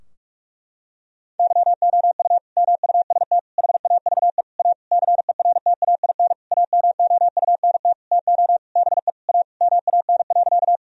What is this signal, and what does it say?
Morse code, heard as a keyed tone, about 42 wpm, 700 Hz. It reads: YOU MUST HAVE A CERTAIN AMOUNT TO BE A MAN.
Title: Sentences from Top 1000 Words - Review Part 4 - 45wpm